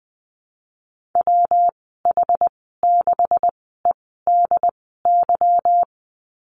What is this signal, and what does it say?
Morse code, heard as a keyed tone, about 20 wpm, spaced as for 15 wpm, 700 Hz. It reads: WH6EDY
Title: 167 Lesson 27g - Number 7 - Call Signs 20wpm